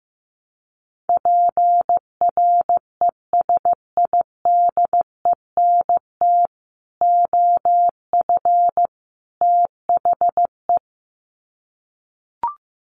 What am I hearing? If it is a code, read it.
Morse code, heard as a keyed tone, about 15 wpm, 700 Hz. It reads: PRESIDENT OF THE
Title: Sets of 3 Words 15wpm